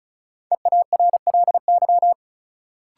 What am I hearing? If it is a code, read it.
Morse code, heard as a keyed tone, about 36 wpm, 700 Hz. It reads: EARLY
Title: Sets of 3 Words - New Word Drill - 35wpm